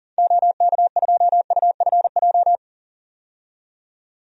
Morse code, heard as a keyed tone, about 38 wpm, 700 Hz. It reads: OK2UFJ
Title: DXCC-100 Call Signs - 6 Characters - 40wpm